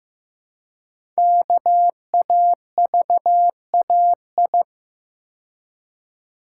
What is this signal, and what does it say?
Morse code, heard as a keyed tone, about 15 wpm, 700 Hz. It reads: KAVAI
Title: Code Groups - 5 Letters - 15wpm